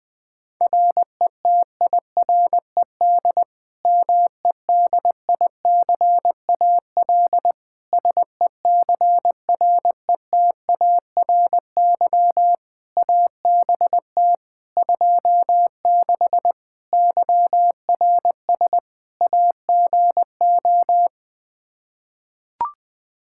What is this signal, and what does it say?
Morse code, heard as a keyed tone, about 20 wpm, 700 Hz. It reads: RETIRED MEDICAL SECRETARY ABT 26 YRS AGO
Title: K1USN SST QSO Practice 20wpm